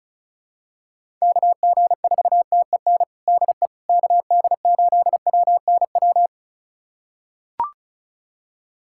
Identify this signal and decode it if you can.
Morse code, heard as a keyed tone, about 34 wpm, 700 Hz. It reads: KG4TEN DE KD8WNW
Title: USA Callsign DE Callsign - 6 Characters - 35wpm